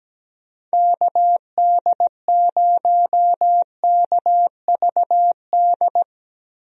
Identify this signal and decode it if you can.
Morse code, heard as a keyed tone, about 17 wpm, 700 Hz. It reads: KD0KVD
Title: USA Call Signs - 6 Characters - Repeated 2x - 17wpm